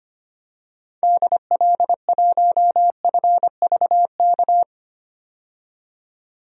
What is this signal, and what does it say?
Morse code, heard as a keyed tone, about 25 wpm, 700 Hz. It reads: DL1FVK
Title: DXCC-100 Call Signs - 6 Characters - 25wpm